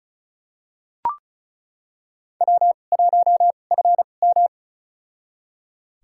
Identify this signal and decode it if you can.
Morse code, heard as a keyed tone, about 36 wpm, spaced as for 25 wpm, 700 Hz. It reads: W1FM
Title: USA Call Signs - 4 Characters - 2x Character Spacing - 35wpm